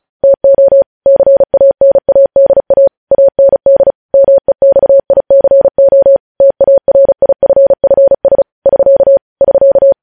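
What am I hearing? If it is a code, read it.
Morse code, heard as a keyed tone, about 36 wpm, 550 Hz. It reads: TO CANADA AND MEXICO TARIFFS <SK> <SK>
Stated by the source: News headlines sent in Morse code at 35 WPM.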